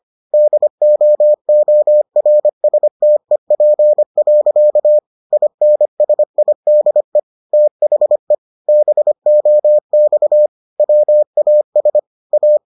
Morse code, heard as a keyed tone, about 25 wpm, 600 Hz. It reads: DOORSTEP. INSIDE THE BOX WAS A